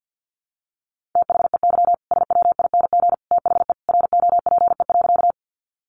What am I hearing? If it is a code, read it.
Morse code, heard as a keyed tone, about 50 wpm, 700 Hz. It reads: THEY SWING THE ROPE.
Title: Easy Phrases 50wpm